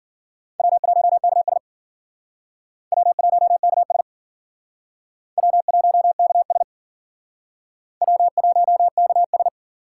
Morse code, heard as a keyed tone, about 51 wpm, 700 Hz. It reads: W1KS W1KS W1KS W1KS